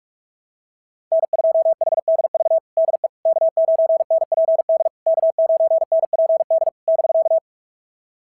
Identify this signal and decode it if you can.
Morse code, heard as a keyed tone, about 44 wpm, 660 Hz. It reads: N2HDV DE K9NPD K9NPD <BK>
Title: SOTA QSO Practice 45wpm